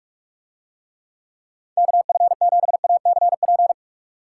Morse code, heard as a keyed tone, about 44 wpm, 700 Hz. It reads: KF7ACP